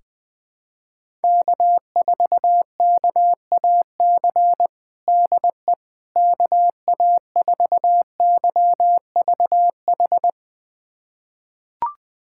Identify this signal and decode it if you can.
Morse code, heard as a keyed tone, about 20 wpm, 700 Hz. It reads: K4KAC DE KA4YVH
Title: USA Callsign DE Callsign 20wpm